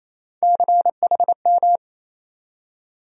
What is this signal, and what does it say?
Morse code, heard as a keyed tone, about 28 wpm, 700 Hz. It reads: CHM
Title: USA Call Sign Postfixes - 3 Letters 28wpm